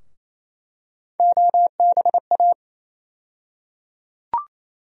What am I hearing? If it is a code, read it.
Morse code, heard as a keyed tone, about 27 wpm, 700 Hz. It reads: OBA
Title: USA Call Sign Postfixes - 3 Letters 28wpm